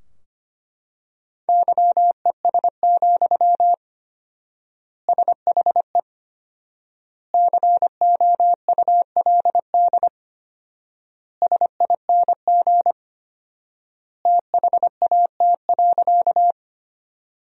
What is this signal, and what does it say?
Morse code, heard as a keyed tone, about 25 wpm, 700 Hz. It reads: YES, SHE COULD SING THAT.
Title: 4x Word-Spacing - Sentences from Top 500 Words - 25wpm